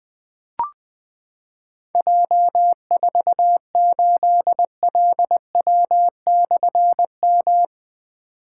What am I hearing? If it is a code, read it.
Morse code, heard as a keyed tone, about 20 wpm, 700 Hz. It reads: J48LW/M
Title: Difficult Call Signs 20wpm